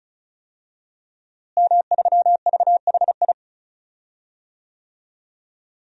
Morse code, heard as a keyed tone, about 36 wpm, 700 Hz. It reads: M3VHI
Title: DXCC-100 Call Signs - 5 Characters - Repeated 2x - 35wpm